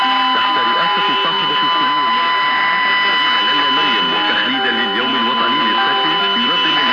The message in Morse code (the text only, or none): none